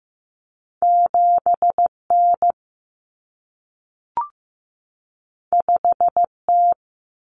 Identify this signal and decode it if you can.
Morse code, heard as a keyed tone, about 15 wpm, 700 Hz. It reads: 7N 5T
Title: Common Cut Numbers - 2-Digit 15wpm